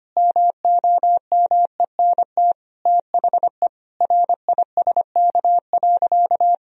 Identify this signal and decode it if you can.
Morse code, heard as a keyed tone, about 25 wpm, 700 Hz. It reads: MOMENT THE RISK.
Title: Sentences from Top 1000 Words - Review Part 4 - 25wpm